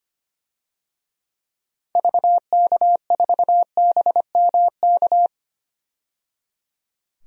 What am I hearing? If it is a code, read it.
Morse code, heard as a keyed tone, about 25 wpm, 700 Hz. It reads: VK4BMK
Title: DXCC-100 Call Signs - 6 Characters - 25wpm